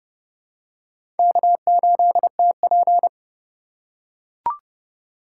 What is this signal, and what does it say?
Morse code, heard as a keyed tone, about 30 wpm, 700 Hz. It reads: K8TP
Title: USA Call Signs - 4x Character Spacing - 30wpm